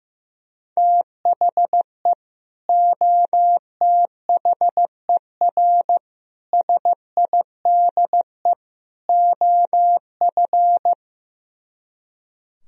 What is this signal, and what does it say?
Morse code, heard as a keyed tone, about 15 wpm, 700 Hz. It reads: THE OTHER SIDE OF
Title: Speed-Racing - Sets of 4 Words 15wpm